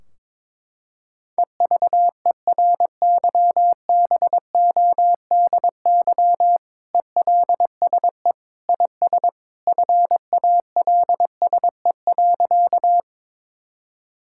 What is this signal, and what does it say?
Morse code, heard as a keyed tone, about 22 wpm, 700 Hz. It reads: EVERYBODY ELSE IS FALSE.
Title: Sentences from Top 1600 Words 22wpm